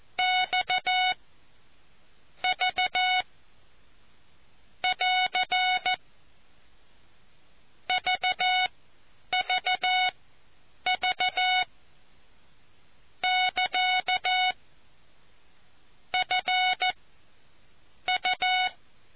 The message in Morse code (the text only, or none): X V <AR> VVV <KA> FU